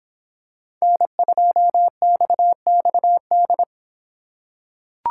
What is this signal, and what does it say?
Morse code, heard as a keyed tone, about 24 wpm, 700 Hz. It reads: N2XXD